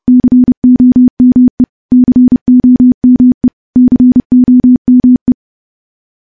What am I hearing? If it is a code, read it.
Morse code, heard as a keyed tone, about 30 wpm, 250 Hz. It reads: COME COME COME
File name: morse / morse-training-30WPM